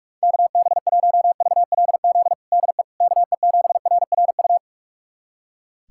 Morse code, heard as a keyed tone, about 47 wpm, 700 Hz. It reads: KB1VLZ DE KE7RRU
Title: USA Callsign DE Callsign 45wpm